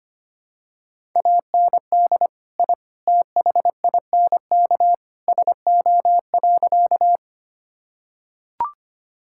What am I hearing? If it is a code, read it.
Morse code, heard as a keyed tone, about 25 wpm, 700 Hz. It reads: AND I THINK SO.
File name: practice / 25wpm / Sentences with 3-Word Phrases - 25wpm